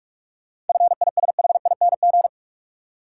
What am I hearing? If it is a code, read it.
Morse code, heard as a keyed tone, about 44 wpm, 700 Hz. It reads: FISHING
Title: Sentences from Top 1600 Words - New Word Drill Alt - 45wpm